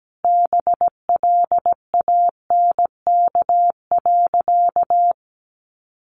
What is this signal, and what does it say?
Morse code, heard as a keyed tone, about 17 wpm, 700 Hz. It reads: BLANK.
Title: Sentences from Top 1500 Words 17wpm